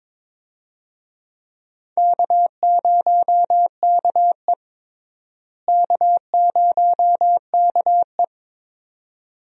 Morse code, heard as a keyed tone, about 22 wpm, 700 Hz. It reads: K0KE K0KE
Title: USA Call Signs - 4 Characters - Repeated 2x - 22wpm